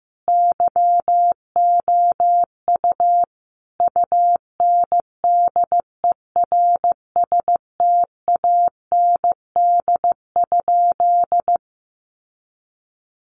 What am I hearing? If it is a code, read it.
Morse code, heard as a keyed tone, about 15 wpm, 700 Hz. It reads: YOU UNDERSTAND?
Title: Sentences from Top 600 Words 15wpm